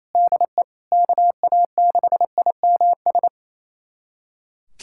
Morse code, heard as a keyed tone, about 28 wpm, 700 Hz. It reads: DE KA6IMS
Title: USA Callsign DE Callsign - 6 Characters - 28wpm